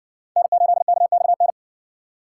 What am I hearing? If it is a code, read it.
Morse code, heard as a keyed tone, about 55 wpm, 700 Hz. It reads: N8KXN